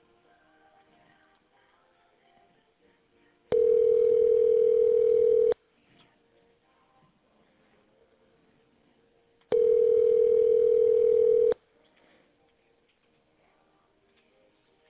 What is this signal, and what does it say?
Ringback tone (North America), 440 + 480 Hz.